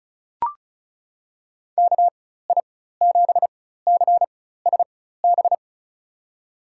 Morse code, heard as a keyed tone, about 36 wpm, spaced as for 16 wpm, 700 Hz. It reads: KI7CSB
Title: USA Call Signs - 6 Characters - 4x Character Spacing - 35wpm